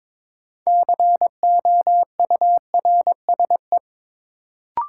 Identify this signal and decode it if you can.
Morse code, heard as a keyed tone, about 22 wpm, 700 Hz. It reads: COURSE
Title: Sentences from Top 500 Words - New Word Drill - 22wpm